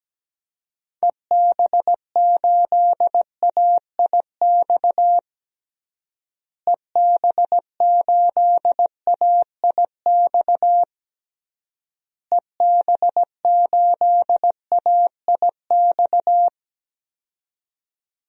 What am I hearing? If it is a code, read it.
Morse code, heard as a keyed tone, about 17 wpm, 700 Hz. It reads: EB8AIX EB8AIX EB8AIX
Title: DXCC-100 Call Signs - 6 Characters - Repeated 3x - 17wpm